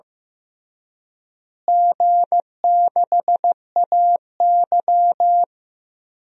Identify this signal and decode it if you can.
Morse code, heard as a keyed tone, about 16 wpm, 700 Hz. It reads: G6AY
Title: DXCC-100 Call Signs - 4 Characters - Repeated 3x - 15wpm